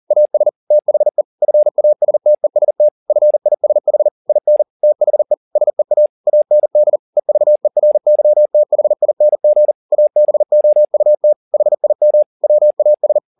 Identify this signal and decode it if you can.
Morse code, heard as a keyed tone, about 40 wpm, 600 Hz. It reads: AS THE FASTEST FISH IN THE SEA AND EVERYTHING ABOUT HIM WAS
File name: the old man and the sea 2 40wpm